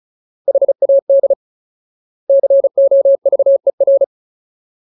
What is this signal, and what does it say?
Morse code, heard as a keyed tone, about 34 wpm, 550 Hz. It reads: HAD COVER
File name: Ultimate Word List / 35WPM Ultimate